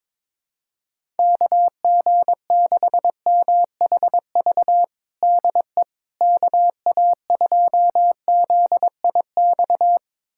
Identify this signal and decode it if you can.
Morse code, heard as a keyed tone, about 22 wpm, 700 Hz. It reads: KG6MHV DE KA2ZIX
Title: USA Callsign DE Callsign 22wpm